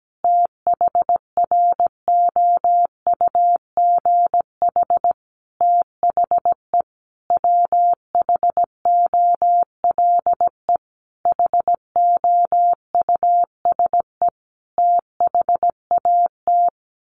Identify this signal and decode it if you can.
Morse code, heard as a keyed tone, about 17 wpm, 700 Hz. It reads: THROUGH THE WHOLE HOUSE THAT
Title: Sentences from Top 1200 Words 17wpm